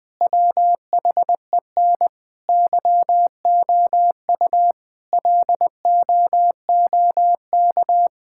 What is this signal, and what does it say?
Morse code, heard as a keyed tone, about 20 wpm, 700 Hz. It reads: WHEN YOU LOOK
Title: Sentences with 4-Word Phrases - 20wpm